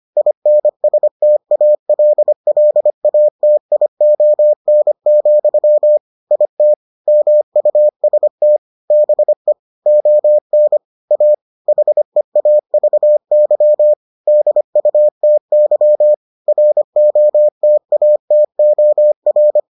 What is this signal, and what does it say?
Morse code, heard as a keyed tone, about 25 wpm, 600 Hz. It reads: INSTALLATION, IT MUST BE ON A HEAVY DUTY ROTATOR